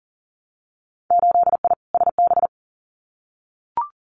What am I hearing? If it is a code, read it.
Morse code, heard as a keyed tone, about 38 wpm, 700 Hz. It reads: 8I SB
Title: Winter Field Day Exchanges Encore 40wpm